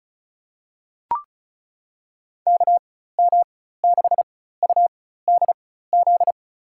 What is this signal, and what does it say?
Morse code, heard as a keyed tone, about 34 wpm, spaced as for 16 wpm, 700 Hz. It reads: KM6UDZ
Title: USA Call Signs - 6 Characters - 4x Character Spacing - 35wpm